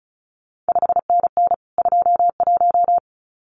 Morse code, heard as a keyed tone, about 36 wpm, 700 Hz. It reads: 5NN 21
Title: CQ WW DX Contest Exchange Encore - 35wpm